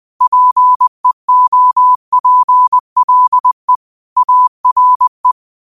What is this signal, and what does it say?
Morse code, heard as a keyed tone, about 20 wpm, 1000 Hz. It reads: PEOPLE ARE